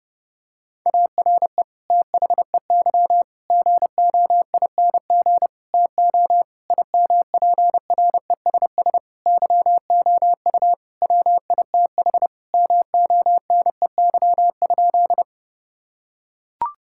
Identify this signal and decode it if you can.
Morse code, heard as a keyed tone, about 30 wpm, 700 Hz. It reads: ARE THEY GOING TO IMPRESS YOU WITH MONEY?
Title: Sentences from Top 1600 Words 30wpm